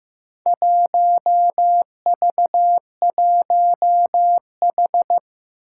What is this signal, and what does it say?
Morse code, heard as a keyed tone, about 15 wpm, 700 Hz. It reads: 1V1H
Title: Four Challenging Letter-Number Combinations 15wpm